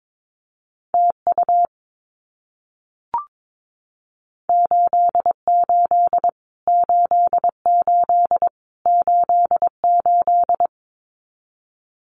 Morse code, heard as a keyed tone, about 22 wpm, 700 Hz. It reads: TU 88 88 88
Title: Abbreviations Repeated 3x - 22wpm